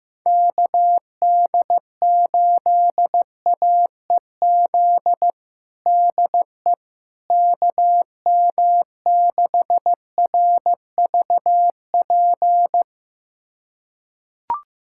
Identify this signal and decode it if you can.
Morse code, heard as a keyed tone, about 16 wpm, 700 Hz. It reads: KD8AEZ DE KM6RVP
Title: USA Callsign DE Callsign - 6 Characters - 15wpm